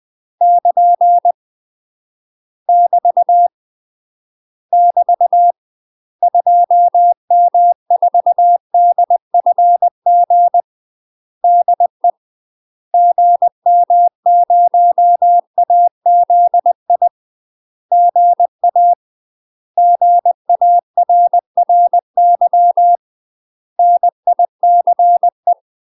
Morse code, heard as a keyed tone, about 20 wpm, 700 Hz. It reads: <KN> <BT> <BT> 2M4DFG DE GM0AZI GA GARRY NICE
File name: UK MASTER QSOs TRAINING / 20WPM / UK MASTER QSOs 0064